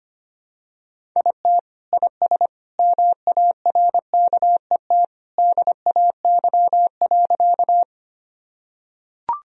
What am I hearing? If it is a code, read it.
Morse code, heard as a keyed tone, about 25 wpm, 700 Hz. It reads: IT IS MARKET DAY.
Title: Sentences from Top 1000 Words - Review Part 3 - 25wpm